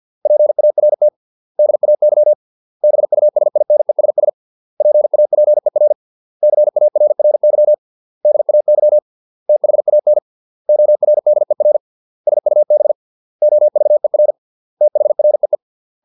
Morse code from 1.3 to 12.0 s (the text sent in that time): DAY BUSINESS PAPER CARRY DAY THAN ORDER